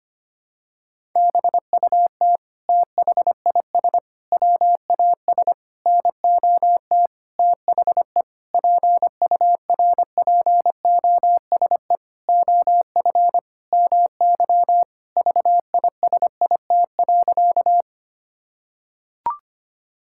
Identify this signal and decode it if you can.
Morse code, heard as a keyed tone, about 25 wpm, 700 Hz. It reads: BUT THIS WAS NOT THE PURPOSE OF MY VISIT.
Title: Sentences from Top 1000 Words - Review Part 6 - 25wpm